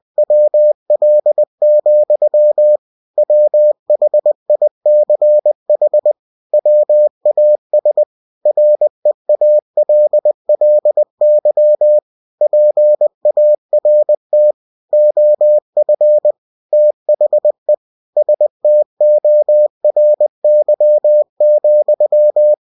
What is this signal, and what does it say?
Morse code, heard as a keyed tone, about 20 wpm, 600 Hz. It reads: WL, WHICH WAS REALLY PART OF THE STORY,
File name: cw-books / winnie-the-pooh / winnie-20wpm-page-0101